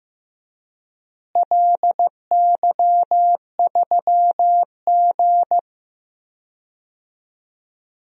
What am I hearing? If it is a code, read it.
Morse code, heard as a keyed tone, about 15 wpm, 700 Hz. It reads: LY3G
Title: DXCC-100 Call Signs - 4 Characters - Repeated 2x - 15wpm